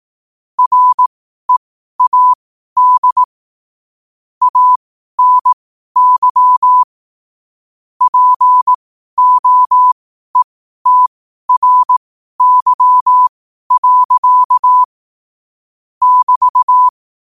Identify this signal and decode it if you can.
Morse code, heard as a keyed tone, about 18 wpm, spaced as for 12 wpm, 1000 Hz. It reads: READ ANY POETRY. <BT>
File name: qotdfiles / 12wpm